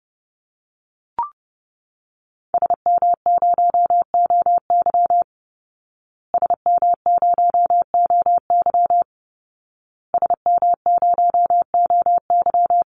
Morse code, heard as a keyed tone, about 30 wpm, 700 Hz. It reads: SM0OY SM0OY SM0OY
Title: Call Signs Repeated 2x - 30wpm